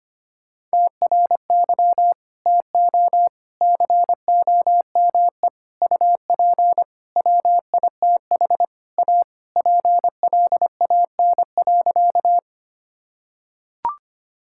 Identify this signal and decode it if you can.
Morse code, heard as a keyed tone, about 25 wpm, 700 Hz. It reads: TRY TO COME UP WITH A PLAN.
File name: practice / 25wpm / Sentences with 4-Word Phrases - 25wpm